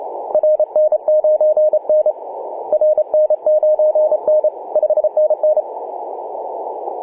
Morse code, heard as a keyed tone, about 28 wpm, 610 Hz, with radio noise. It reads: RN9N RN9N 5NN
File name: posts / oct-19-0756z-vk9xsp-24-901mhz-cw